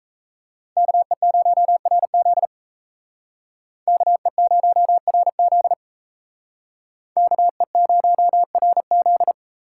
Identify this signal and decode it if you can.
Morse code, heard as a keyed tone, about 35 wpm, 700 Hz. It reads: KE0RZ KE0RZ KE0RZ